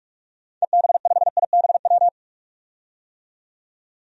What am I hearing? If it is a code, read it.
Morse code, heard as a keyed tone, about 46 wpm, 700 Hz. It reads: EB5IBW
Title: DXCC-100 Call Signs - 6 Characters - 45wpm